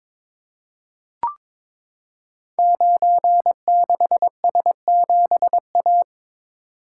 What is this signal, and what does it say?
Morse code, heard as a keyed tone, about 22 wpm, 700 Hz. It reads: 96S7A
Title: Code Groups - 5 Letter:Numbers - 22wpm